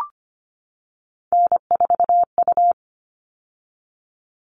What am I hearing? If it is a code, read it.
Morse code, heard as a keyed tone, about 25 wpm, 700 Hz. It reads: N4U
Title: All Cut Numbers - 3-Digit 25wpm